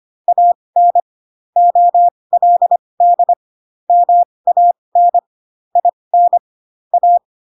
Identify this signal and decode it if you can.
Morse code, heard as a keyed tone, about 25 wpm, spaced as for 20 wpm, 700 Hz. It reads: AN OLD MAN IN A
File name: Pinocchio / Pinocchio ch6 15 25 ik1hge